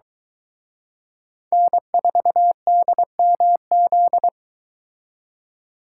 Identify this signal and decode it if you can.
Morse code, heard as a keyed tone, about 23 wpm, 700 Hz. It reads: N4DMZ